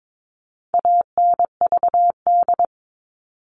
Morse code, heard as a keyed tone, about 22 wpm, 700 Hz. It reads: ANVD
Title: All Cut Numbers - 4-Digit 22wpm